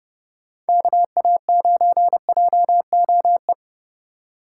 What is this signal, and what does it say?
Morse code, heard as a keyed tone, about 32 wpm, 700 Hz. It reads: KA9JOE
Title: USA Call Signs - 6 Characters - 3x Character Spacing - 30wpm